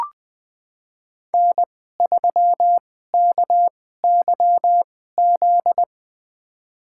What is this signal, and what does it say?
Morse code, heard as a keyed tone, about 20 wpm, spaced as for 14 wpm, 700 Hz. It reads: N3KYZ
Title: USA Call Signs - 2x Character Spacing - 20wpm